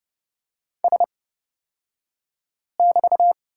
Morse code, heard as a keyed tone, about 30 wpm, spaced as for 5 wpm, 700 Hz. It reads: S<BT>
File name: Lesson 38d - Letter q - Two Characters 30wpm